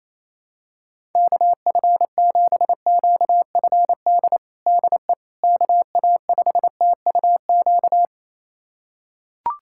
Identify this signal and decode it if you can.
Morse code, heard as a keyed tone, about 28 wpm, 700 Hz. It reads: KF7QFD DE KA5TUQ
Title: USA Callsign DE Callsign - 6 Characters - 28wpm